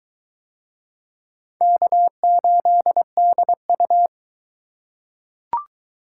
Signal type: Morse code

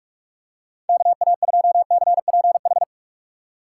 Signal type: Morse code

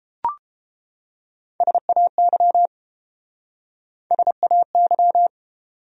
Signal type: Morse code

